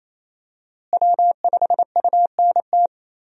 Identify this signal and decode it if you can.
Morse code, heard as a keyed tone, about 28 wpm, 700 Hz. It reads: W5UNT
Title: USA Call Signs - 5 Characters - Repeated 3x - 28wpm